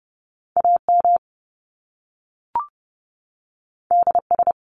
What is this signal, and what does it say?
Morse code, heard as a keyed tone, about 30 wpm, 700 Hz. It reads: AM DS
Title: Top Two Letter Combinations in Words 30wpm